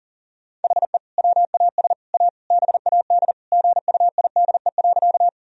Morse code, heard as a keyed tone, about 40 wpm, 700 Hz. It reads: HE WAS A BAD GUIDE.
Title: Sentences from Top 1000 Words - Review Part 3 - 40wpm